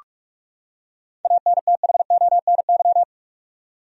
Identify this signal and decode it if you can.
Morse code, heard as a keyed tone, about 46 wpm, 700 Hz. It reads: ANTHONY
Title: USA Common Names - 45wpm